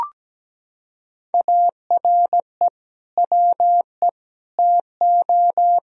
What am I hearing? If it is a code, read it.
Morse code, heard as a keyed tone, about 17 wpm, 700 Hz. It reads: ARE WE TO
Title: Sentences from Top 1000 Words - Review Part 1 - 17wpm